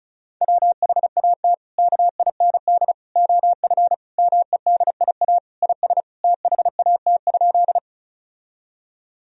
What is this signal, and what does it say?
Morse code, heard as a keyed tone, about 36 wpm, 700 Hz. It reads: WHAT KIND OF MEDIA IS THAT?
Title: Sentences from Top 1000 Words - Review Part 4 - 35wpm